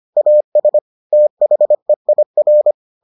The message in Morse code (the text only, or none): AS THEIR